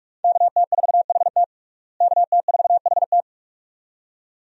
Morse code, heard as a keyed tone, about 46 wpm, 700 Hz. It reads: KT4HT KT4HT
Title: USA Call Signs - 5 Characters - Repeated 2x - 45wpm